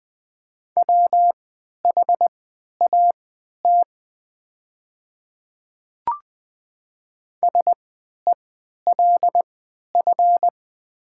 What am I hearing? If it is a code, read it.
Morse code, heard as a keyed tone, about 20 wpm, spaced as for 11 wpm, 700 Hz. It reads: WHAT SELF
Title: 3x Character Spacing - Top 300 Words - 20wpm